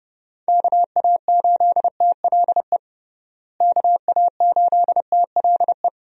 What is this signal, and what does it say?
Morse code, heard as a keyed tone, about 30 wpm, 700 Hz. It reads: KA8TLE KA8TLE
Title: USA Call Signs - 6 Characters - Repeated 2x - 30wpm